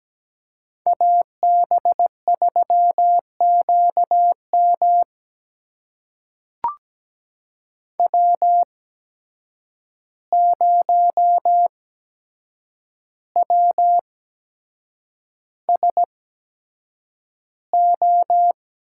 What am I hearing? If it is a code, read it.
Morse code, heard as a keyed tone, about 17 wpm, spaced as for 7 wpm, 700 Hz. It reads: AB3QM W0WSO
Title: USA Call Signs - 5 Characters - 8x Character Spacing - 17wpm